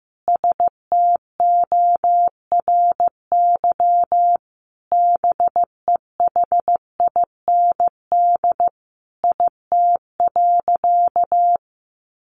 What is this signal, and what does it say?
Morse code, heard as a keyed tone, about 14 wpm, 700 Hz. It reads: STORY BEHIND IT.
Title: Sentences from Top 500 Words 15wpm